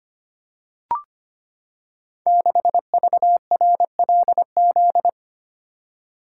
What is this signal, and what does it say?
Morse code, heard as a keyed tone, about 25 wpm, 700 Hz. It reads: 6VRLZ
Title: Code Groups - 5 Letter:Numbers - 25wpm